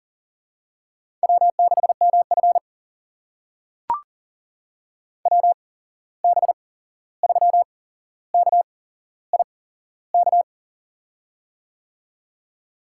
Morse code, heard as a keyed tone, about 39 wpm, spaced as for 15 wpm, 700 Hz. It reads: W6MF WB3KIK
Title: USA Call Signs - 8x Character Spacing - 40wpm